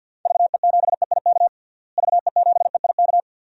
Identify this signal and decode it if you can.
Morse code, heard as a keyed tone, about 49 wpm, 700 Hz. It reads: VE7EIK VE7EIK
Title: DXCC-100 Call Signs - 6 Characters - Repeated 3x - 50wpm